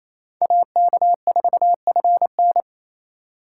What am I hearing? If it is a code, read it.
Morse code, heard as a keyed tone, about 35 wpm, 700 Hz. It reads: AK4FN